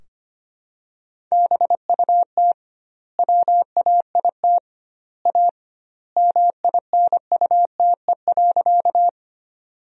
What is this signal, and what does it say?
Morse code, heard as a keyed tone, about 25 wpm, 700 Hz. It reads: BUT WAIT A MINUTE.
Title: 2x Word-Spacing - Sentences from Top 500 Words - 25wpm